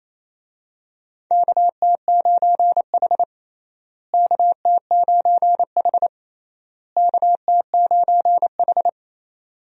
Morse code, heard as a keyed tone, about 27 wpm, 700 Hz. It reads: KT9H KT9H KT9H
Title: USA Call Signs - 4 Characters - Repeated 3x - 28wpm